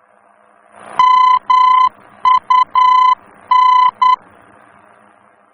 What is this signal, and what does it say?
Morse code, heard as a keyed tone, about 10 wpm, 1010 Hz. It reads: MUN
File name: germany / mx / MUN-D-VOR